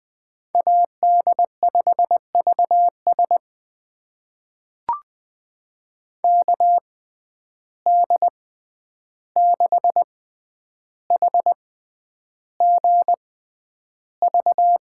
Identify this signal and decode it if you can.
Morse code, heard as a keyed tone, about 21 wpm, spaced as for 10 wpm, 700 Hz. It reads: AD5VS KD6HGV
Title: USA Call Signs - 6x Character Spacing - 20wpm